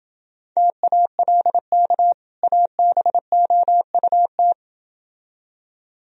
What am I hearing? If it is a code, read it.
Morse code, heard as a keyed tone, about 29 wpm, 700 Hz. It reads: TALK ABOUT